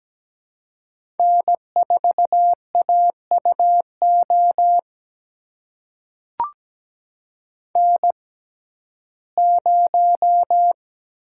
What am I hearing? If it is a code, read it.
Morse code, heard as a keyed tone, about 17 wpm, spaced as for 12 wpm, 700 Hz. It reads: N4AUO N0
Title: USA Call Signs - 5 Characters - 6x Character Spacing - 17wpm